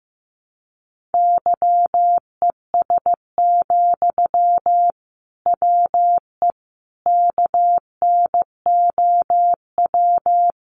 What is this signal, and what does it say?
Morse code, heard as a keyed tone, about 15 wpm, 700 Hz. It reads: YES, WE KNOW
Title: Sentences from Top 500 Words 15wpm